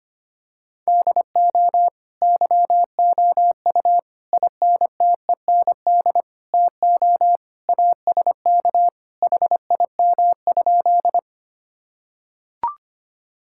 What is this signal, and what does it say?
Morse code, heard as a keyed tone, about 25 wpm, 700 Hz. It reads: DO YOU INTEND TO ASK HIM?
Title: Sentences from Top 1600 Words 25wpm